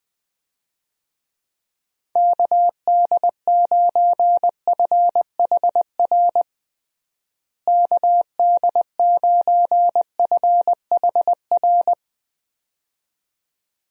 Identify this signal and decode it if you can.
Morse code, heard as a keyed tone, about 20 wpm, 700 Hz. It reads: KD9FHR KD9FHR
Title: USA Call Signs - 6 Characters - Repeated 2x - 20wpm